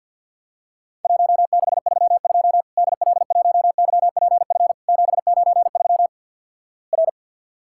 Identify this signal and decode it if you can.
Morse code, heard as a keyed tone, about 50 wpm, 700 Hz. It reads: 1632 DL1YPF 793 R
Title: Worked All Europe DX Contest - QTC Exchange with Repeats - 50wpm